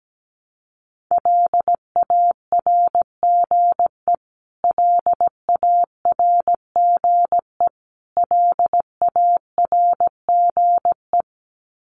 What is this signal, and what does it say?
Morse code, heard as a keyed tone, about 17 wpm, 700 Hz. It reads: LARGE LARGE LARGE
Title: Top 500 Words Repeated 17wpm